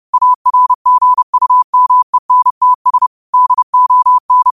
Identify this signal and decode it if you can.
Morse code, heard as a keyed tone, about 30 wpm, 1000 Hz. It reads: ARGUMENTS DON
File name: qotdfiles / 30wpm